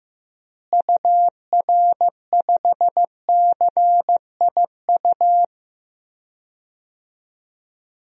Morse code, heard as a keyed tone, about 15 wpm, 700 Hz. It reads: UR5CIU
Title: DXCC-100 Call Signs - 6 Characters - Repeated 3x - 15wpm